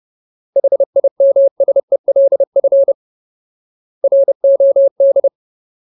Morse code, heard as a keyed tone, about 30 wpm, 550 Hz. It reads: HIMSELF ROD
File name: Ultimate Word List / 30WPM Ultimate